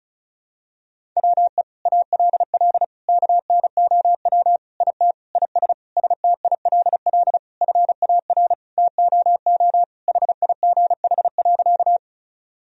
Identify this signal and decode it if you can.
Morse code, heard as a keyed tone, about 36 wpm, 700 Hz. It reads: WE ALL KNOW IT IS STILL FAR TOO HIGH.
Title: Sentences from Top 1000 Words - Review Part 1 - 35wpm